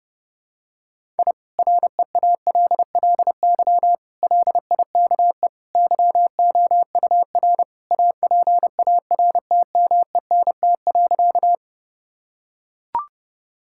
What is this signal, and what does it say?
Morse code, heard as a keyed tone, about 30 wpm, 700 Hz. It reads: I REALLY LIKE YOUR APARTMENT.
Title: Sentences from Top 1000 Words - Review Part 6 - 30wpm